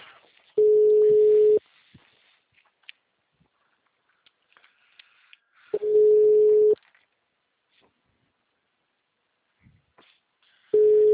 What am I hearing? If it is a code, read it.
Ringback tone (Europe, 425 Hz).
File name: out- -2007- - - .604